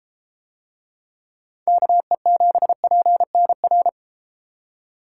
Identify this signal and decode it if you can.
Morse code, heard as a keyed tone, about 32 wpm, 700 Hz. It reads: KE7PNR